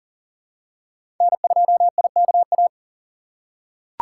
Morse code, heard as a keyed tone, about 39 wpm, 700 Hz. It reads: N2IKA
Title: USA Call Signs - 5 Characters - 2x Character Spacing - 40wpm